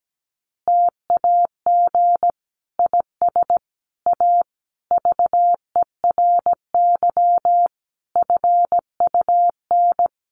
Morse code, heard as a keyed tone, about 17 wpm, 700 Hz. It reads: TAG IS A VERY FUN
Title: Additional Easy Phrases 17wpm